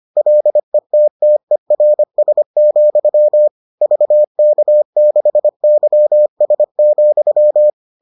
Morse code, heard as a keyed tone, about 25 wpm, 600 Hz. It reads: LETTERS, VK6YS,